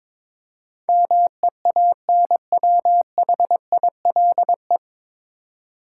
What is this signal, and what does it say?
Morse code, heard as a keyed tone, about 22 wpm, 700 Hz. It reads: MEANWHILE
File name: 9 Letter Words - First 3 Letters Given - 22wpm